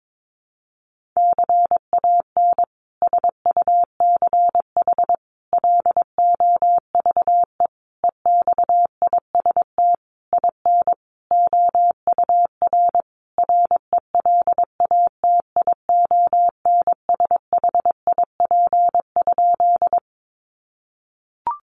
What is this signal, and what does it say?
Morse code, heard as a keyed tone, about 22 wpm, 700 Hz. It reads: CAN SUCH LOVE EXIST IN OUR RELATIONSHIP?
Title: Sentences from Top 1100 Words 22wpm